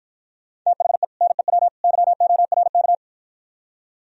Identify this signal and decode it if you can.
Morse code, heard as a keyed tone, about 53 wpm, 700 Hz. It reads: THE NEW YORK